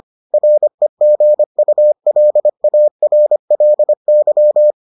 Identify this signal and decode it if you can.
Morse code, heard as a keyed tone, about 25 wpm, 600 Hz. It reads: REGULARLY